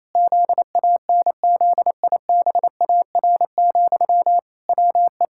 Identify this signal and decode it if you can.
Morse code, heard as a keyed tone, about 28 wpm, 700 Hz. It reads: ZANZIBAR, WE
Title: Pangram Challenge v2 - 28wpm